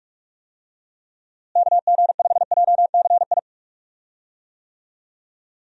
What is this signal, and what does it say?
Morse code, heard as a keyed tone, about 47 wpm, 700 Hz. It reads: KG5JCI
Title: USA Call Signs - 6 Characters - Repeated 3x - 45wpm